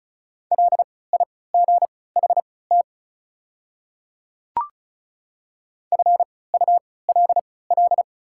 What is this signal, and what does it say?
Morse code, heard as a keyed tone, about 36 wpm, spaced as for 20 wpm, 700 Hz. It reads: LIGHT FULL
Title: 3x Character Spacing - Top 500 Words - 35wpm